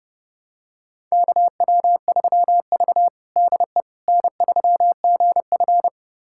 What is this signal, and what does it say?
Morse code, heard as a keyed tone, about 30 wpm, 700 Hz. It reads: KW3V DE N3GF
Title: USA Callsign DE Callsign - 4 Characters - 30wpm